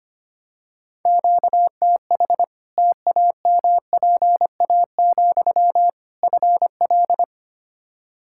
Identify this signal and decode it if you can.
Morse code, heard as a keyed tone, about 25 wpm, 700 Hz. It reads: QTH TAMPA, FL
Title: Practice QSO - Exchange 1 - Beginning - 25wpm